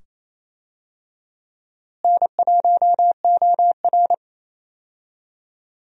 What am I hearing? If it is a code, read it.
Morse code, heard as a keyed tone, about 28 wpm, 700 Hz. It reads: N1OR
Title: USA Call Signs - 4 Characters - Repeated 2x - 28wpm